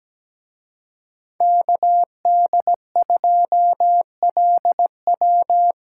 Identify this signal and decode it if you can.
Morse code, heard as a keyed tone, about 20 wpm, 700 Hz. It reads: KD2LW
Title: Speed Racing - USA Call Signs 17wpm